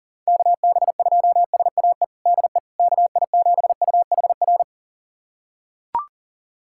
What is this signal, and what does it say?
Morse code, heard as a keyed tone, about 40 wpm, 700 Hz. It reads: KB2SAE DE KI7UHR
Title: USA Callsign DE Callsign - 6 Characters - 40wpm